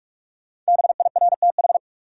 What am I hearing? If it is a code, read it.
Morse code, heard as a keyed tone, about 46 wpm, 700 Hz. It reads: BIRTH
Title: Sentences from Top 800 Words - New Word Drill - 45wpm